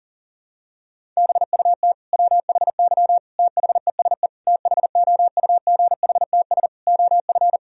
Morse code, heard as a keyed tone, about 40 wpm, 700 Hz. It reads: BUT WHY THESE THOUGHTS OF
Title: Sentences from Top 1000 Words - Review Part 1 - 40wpm